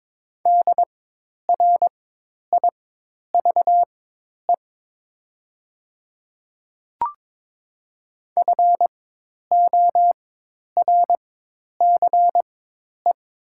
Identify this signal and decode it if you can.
Morse code, heard as a keyed tone, about 22 wpm, spaced as for 10 wpm, 700 Hz. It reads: DRIVE FORCE
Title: 4x Character Spacing - Top 500 Words - 22wpm